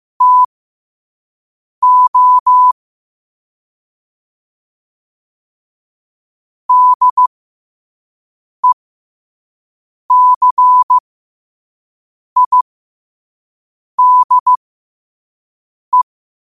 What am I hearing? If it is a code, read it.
Morse code, heard as a keyed tone, about 15 wpm, spaced as for 5 wpm, 1000 Hz. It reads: TO DECIDE
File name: qotdfiles / 5wpm